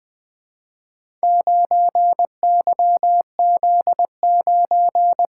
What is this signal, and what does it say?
Morse code, heard as a keyed tone, about 20 wpm, 700 Hz. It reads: 9YZ9
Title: Four Letter-Number Combinations - 20wpm